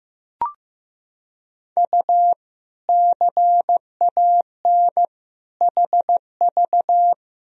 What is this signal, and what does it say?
Morse code, heard as a keyed tone, about 16 wpm, 700 Hz. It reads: U CAN HV
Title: Sentences from Top 100 Words and Abbreviations 15wpm